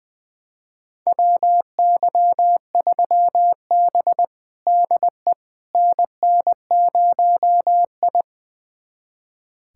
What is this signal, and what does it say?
Morse code, heard as a keyed tone, about 20 wpm, 700 Hz. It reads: WY3B DE NN0I
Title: USA Callsign DE Callsign - 4 Characters - 20wpm